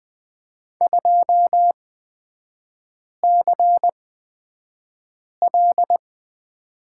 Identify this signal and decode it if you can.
Morse code, heard as a keyed tone, about 20 wpm, spaced as for 5 wpm, 700 Hz. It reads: 2CL
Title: 255 Lesson 41e - Period - Three Characters 20wpm